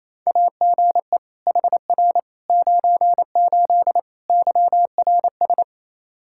Morse code, heard as a keyed tone, about 27 wpm, 700 Hz. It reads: AGE HR 98 YRS
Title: K1USN SST QSO Practice 28wpm